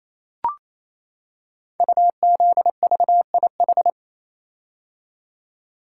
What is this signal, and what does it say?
Morse code, heard as a keyed tone, about 28 wpm, 700 Hz. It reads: UZVIH
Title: Code Groups - 5 Letters - 28wpm